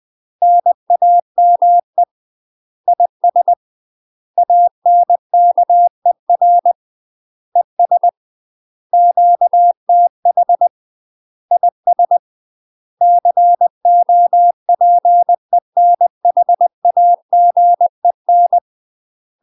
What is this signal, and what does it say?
Morse code, heard as a keyed tone, about 20 wpm, 700 Hz. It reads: NAME IS ANKER ES QTH IS COPENHAGEN
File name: UK MASTER QSOs TRAINING / 20WPM / UK MASTER QSOs 0042